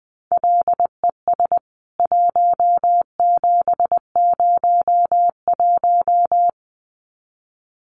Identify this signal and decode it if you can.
Morse code, heard as a keyed tone, about 20 wpm, 700 Hz. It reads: LES 1701
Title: CWT Exchange 20wpm